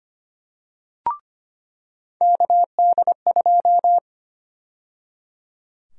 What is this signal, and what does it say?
Morse code, heard as a keyed tone, about 25 wpm, 700 Hz. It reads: KD2
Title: USA Call Sign Prefixes - 25wpm